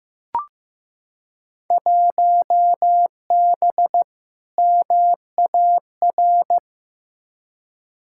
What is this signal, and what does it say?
Morse code, heard as a keyed tone, about 15 wpm, 700 Hz. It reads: 1B MAR
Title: Field Day Exchanges Encore - 15wpm